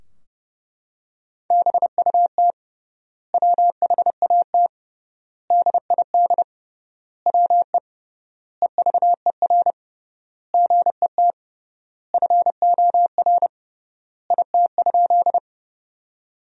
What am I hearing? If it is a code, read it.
Morse code, heard as a keyed tone, about 30 wpm, 700 Hz. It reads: BUT WHAT DID WE EVER GET FOR IT?
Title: 3x Word-Spacing - Sentences from Top 400 Words - 30wpm